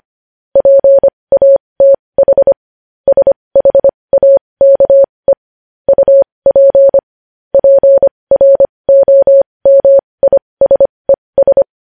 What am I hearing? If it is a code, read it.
Morse code, heard as a keyed tone, about 25 wpm, spaced as for 20 wpm, 550 Hz. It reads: PATH SHAKE UP PROMISES
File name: podcast / News headlines-1x-25x20